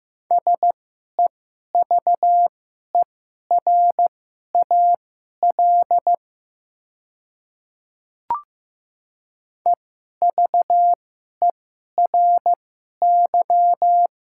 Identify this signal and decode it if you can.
Morse code, heard as a keyed tone, about 15 wpm, spaced as for 11 wpm, 690 Hz. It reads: SEVERAL EVERY
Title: 2x Character Spacing - Top 500 Words - 15wpm